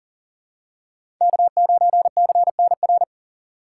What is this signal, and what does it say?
Morse code, heard as a keyed tone, about 40 wpm, 700 Hz. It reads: K9CNR
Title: USA Call Signs - 5 Characters - 2x Character Spacing - 40wpm